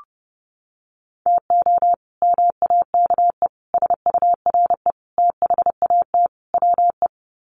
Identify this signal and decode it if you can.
Morse code, heard as a keyed tone, about 30 wpm, 700 Hz. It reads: TO MAKE SURE THAT WE
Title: Sets of 5 Words Encore 30wpm